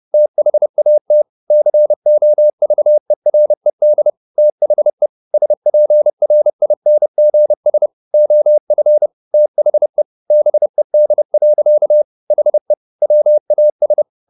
Morse code, heard as a keyed tone, about 30 wpm, 600 Hz. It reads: THAT COVERED THE SPRINGS OF THE BED. HE WAS
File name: the old man and the sea / the old man and the sea 1 30wpm